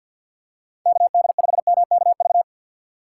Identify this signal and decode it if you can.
Morse code, heard as a keyed tone, about 47 wpm, 700 Hz. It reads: KD5MKV
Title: USA Call Signs - 6 Characters - 6x Character Spacing - 50wpm